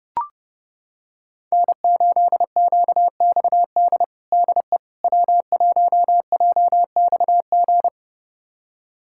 Morse code, heard as a keyed tone, about 32 wpm, 700 Hz. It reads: N8QXD DE W1JXG
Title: USA Callsign DE Callsign 30wpm